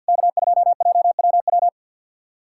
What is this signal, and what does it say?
Morse code, heard as a keyed tone, about 52 wpm, 690 Hz. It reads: K2JWW
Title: SOTA QSO Practice 50wpm